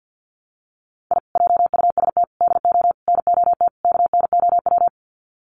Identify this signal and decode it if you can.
Morse code, heard as a keyed tone, about 50 wpm, 700 Hz. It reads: I JUST DO NOT KNOW
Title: Sets of 5 Words Encore 50wpm